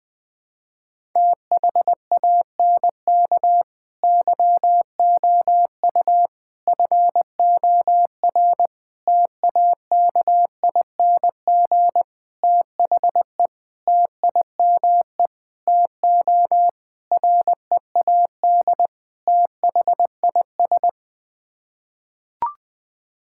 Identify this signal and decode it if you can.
Morse code, heard as a keyed tone, about 20 wpm, 700 Hz. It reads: THANK YOU FOR TAKING THE TIME TO READ THIS
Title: Sets of 9 Words - 20wpm